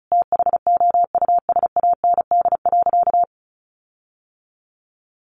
Morse code, heard as a keyed tone, about 34 wpm, 700 Hz. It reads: THOUSAND.
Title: Sentences from Top 1300 Words 35wpm